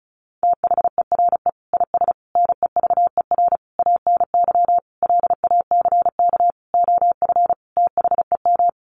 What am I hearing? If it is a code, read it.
Morse code, heard as a keyed tone, about 34 wpm, 700 Hz. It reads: THERE IS NEVER ANY LACK OF THEM
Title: Sentences from Top 1100 Words 35wpm